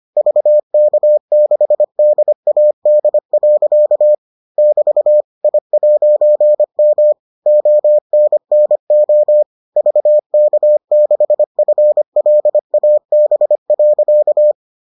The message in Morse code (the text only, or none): VK6DAD. <BT> I'M ONNO VK6FLAB.